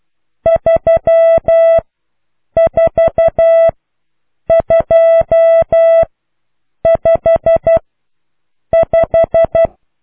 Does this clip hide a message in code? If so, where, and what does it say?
Yes: Morse code from 0.5 to 9.7 s, heard as a keyed tone, about 12 wpm, spaced as for 7 wpm, 660 Hz. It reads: 34255